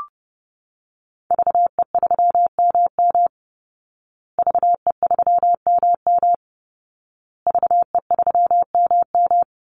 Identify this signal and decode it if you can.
Morse code, heard as a keyed tone, about 30 wpm, 700 Hz. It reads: VE3MM VE3MM VE3MM
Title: Call Signs Repeated 2x - 30wpm